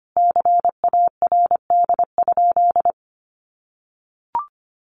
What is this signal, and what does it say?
Morse code, heard as a keyed tone, about 25 wpm, 700 Hz. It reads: CARD?
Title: Sentences from Top 1100 Words 25wpm